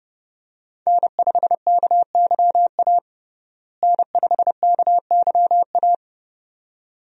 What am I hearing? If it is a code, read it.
Morse code, heard as a keyed tone, about 30 wpm, 700 Hz. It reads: N5KYA N5KYA
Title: USA Call Signs - 5 Characters - Repeated 2x - 30wpm